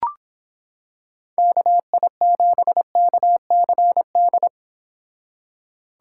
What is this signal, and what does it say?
Morse code, heard as a keyed tone, about 26 wpm, 700 Hz. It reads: KI7KCD